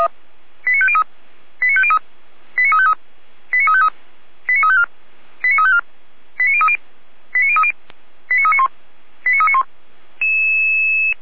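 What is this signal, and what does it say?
Five-tone selective call (ZVEI), heard as quick runs of five tones, tones of about 70 ms. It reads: 89503 89503 89353 89353 89354 89354 89030 89030 89391 89391
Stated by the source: ZVEI five-tone call received by SDR, from feuerwehr-orlamuende.org (fire brigade Orlamünde).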